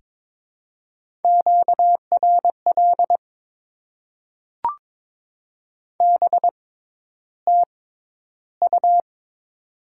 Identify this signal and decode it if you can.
Morse code, heard as a keyed tone, about 22 wpm, spaced as for 11 wpm, 700 Hz. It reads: QRL BTU
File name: Top 20 QSO Elements - 6x Character Spacing - 22wpm